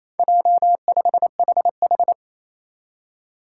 Morse code, heard as a keyed tone, about 28 wpm, 700 Hz. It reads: J5HH
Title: Four Challenging Letter-Number Combinations 28wpm